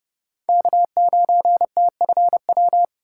Morse code, heard as a keyed tone, about 32 wpm, 700 Hz. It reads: K9TFW
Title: USA Call Signs - 5 Characters - 3x Character Spacing - 30wpm